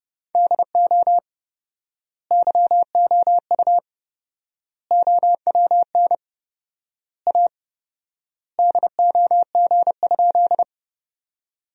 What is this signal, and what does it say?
Morse code, heard as a keyed tone, about 30 wpm, 700 Hz. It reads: DO YOU OWN A DOG?
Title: 4x Word-Spacing - Sentences from Top 400 Words - 30wpm